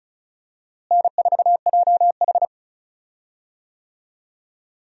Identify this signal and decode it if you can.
Morse code, heard as a keyed tone, about 36 wpm, 700 Hz. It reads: N4JH
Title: USA Call Signs - 4 Characters - Repeated 3x - 35wpm